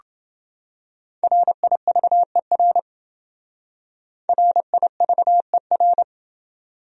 Morse code, heard as a keyed tone, about 29 wpm, 700 Hz. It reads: RIVER RIVER